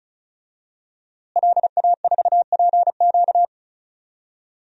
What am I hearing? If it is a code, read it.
Morse code, heard as a keyed tone, about 34 wpm, 700 Hz. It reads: LA4PQ
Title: DXCC-100 Call Signs - 5 Characters - Repeated 2x - 35wpm